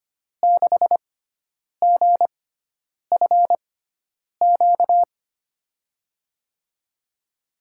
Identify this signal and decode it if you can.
Morse code, heard as a keyed tone, about 25 wpm, spaced as for 9 wpm, 700 Hz. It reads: 6GFQ
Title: USA Call Signs - 6 Characters - 6x Character Spacing - 25wpm